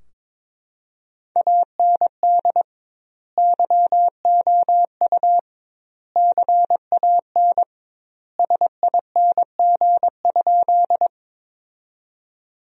Morse code, heard as a keyed tone, about 22 wpm, 700 Hz. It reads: AND YOU CAN SING?
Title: 2x Word-Spacing - Sentences from Top 400 Words - 22wpm